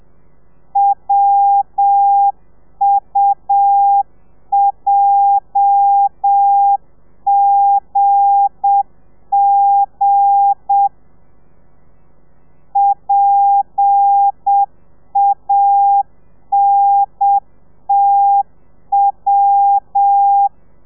Morse code, heard as a keyed tone, about 7 wpm, 800 Hz. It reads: WUJGG PANTW
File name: telegrafia / i2tfj / Lez8 35